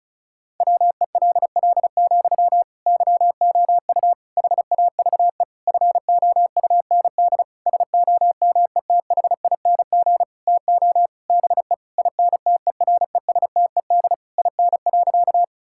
Morse code, heard as a keyed tone, about 34 wpm, 700 Hz. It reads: WELL, YOU HAVE FOUND SOMETHING TO BE INTERESTED IN.
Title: Sentences from Top 1000 Words - Review Part 6 - 35wpm